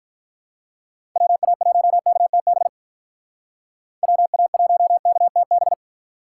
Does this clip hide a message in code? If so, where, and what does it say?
Yes: Morse code from 1.2 to 5.7 s, heard as a keyed tone, about 46 wpm, 700 Hz. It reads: WA1KTB WA1KTB